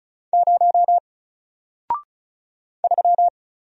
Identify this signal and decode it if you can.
Morse code, heard as a keyed tone, about 36 wpm, 700 Hz. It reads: 0 3
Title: Single Letter-Number 35wpm